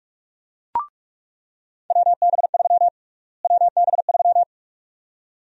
Morse code, heard as a keyed tone, about 46 wpm, 700 Hz. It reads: WB3 WB3
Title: USA Call Sign Prefixes - Repeated 2x - 45wpm